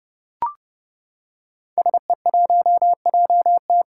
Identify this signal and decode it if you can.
Morse code, heard as a keyed tone, about 30 wpm, 700 Hz. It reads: SE1JT
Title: Five Challenging Letter-Number Combinations 30wpm